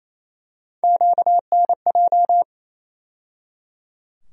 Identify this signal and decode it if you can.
Morse code, heard as a keyed tone, about 28 wpm, 700 Hz. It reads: QNJ
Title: USA Call Sign Postfixes - 3 Letters 28wpm